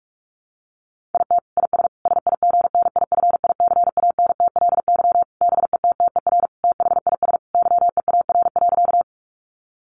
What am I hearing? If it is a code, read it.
Morse code, heard as a keyed tone, about 44 wpm, 700 Hz. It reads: IT IS SIGNIFICANTLY BETTER THIS YEAR.
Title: Sentences from Top 1500 Words 45wpm